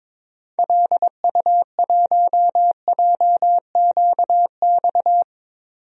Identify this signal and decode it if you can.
Morse code, heard as a keyed tone, about 22 wpm, 700 Hz. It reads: LU1JQX
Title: DXCC-100 Call Signs - 6 Characters - 22wpm